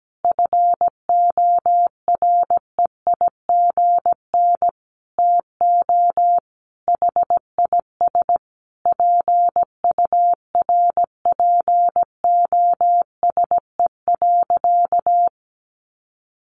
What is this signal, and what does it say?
Morse code, heard as a keyed tone, about 17 wpm, 700 Hz. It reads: FOREIGN TO HIS PURPOSE.
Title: Sentences from Top 1400 Words 17wpm